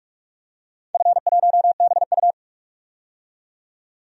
Morse code, heard as a keyed tone, about 44 wpm, 690 Hz. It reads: F1BU
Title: DXCC-100 Call Signs - 4 Characters - 45wpm